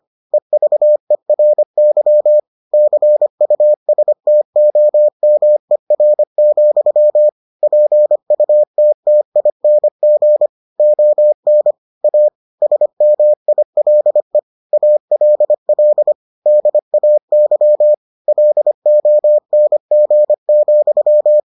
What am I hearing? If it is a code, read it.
Morse code, heard as a keyed tone, about 25 wpm, 600 Hz. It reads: EVERY CUSTOMER, PUTTING ON A SMILE ALL DAY LONG,